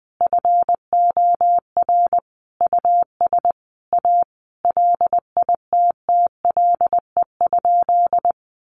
Morse code, heard as a keyed tone, about 20 wpm, 700 Hz. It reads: FOR US A LITTLE?
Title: Sentences from Top 900 Words 20wpm